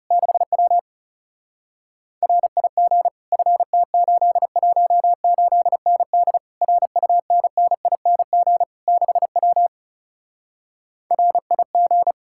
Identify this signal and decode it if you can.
Morse code, heard as a keyed tone, about 40 wpm, 700 Hz. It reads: 6W RIG FT818ND RUNNING 6W RIG